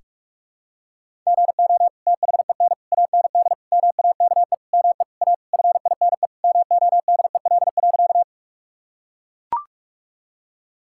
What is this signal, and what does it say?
Morse code, heard as a keyed tone, about 41 wpm, 700 Hz. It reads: GO THEN AND MAKE ME A FINE MODEL.
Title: Sentences from Top 1000 Words - Review Part 5 - 45wpm